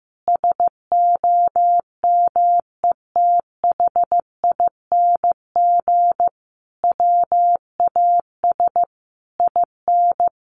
Morse code, heard as a keyed tone, about 15 wpm, 700 Hz. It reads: SOMETHING WAS IN
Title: Sentences from Top 1400 Words 15wpm